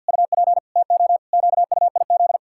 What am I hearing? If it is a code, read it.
Morse code, heard as a keyed tone, about 50 wpm, 700 Hz. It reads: UP TO QUIZ
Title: Pangram Challenge v2 - 50wpm